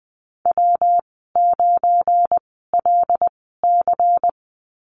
Morse code, heard as a keyed tone, about 20 wpm, spaced as for 14 wpm, 700 Hz. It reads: W9LC
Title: 238 Lesson 38g - Letter q - Call Signs 20wpm